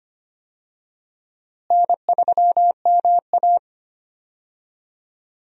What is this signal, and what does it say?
Morse code, heard as a keyed tone, about 25 wpm, 700 Hz. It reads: N3MA
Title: USA Call Signs - 4 Characters - Repeated 3x - 25wpm